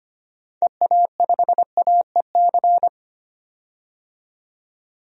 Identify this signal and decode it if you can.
Morse code, heard as a keyed tone, about 25 wpm, 700 Hz. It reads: EA5AEC
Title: DXCC-100 Call Signs - 6 Characters - 25wpm